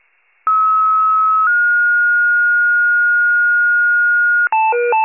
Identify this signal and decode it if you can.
Two-tone page, heard as a one-second tone, then a three-second tone, about 1290 then 1430 Hz. It is heard from 0.5 to 4.5 s.
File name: pager / 84 FIRE / wcalert